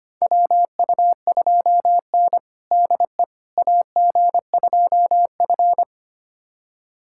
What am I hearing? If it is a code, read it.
Morse code, heard as a keyed tone, about 25 wpm, 700 Hz. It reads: WU2N DE AG2F
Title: USA Callsign DE Callsign - 4 Characters - 25wpm